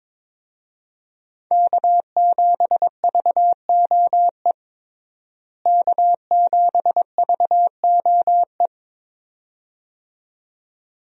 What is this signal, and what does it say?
Morse code, heard as a keyed tone, about 22 wpm, 700 Hz. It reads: K7VOE K7VOE
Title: USA Call Signs - 5 Characters - Repeated 2x - 22wpm